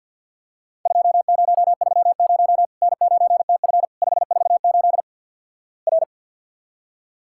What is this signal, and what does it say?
Morse code, heard as a keyed tone, about 50 wpm, 700 Hz. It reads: 2030 N9TF 548 R
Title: Worked All Europe DX Contest - QTC Exchange with Repeats - 50wpm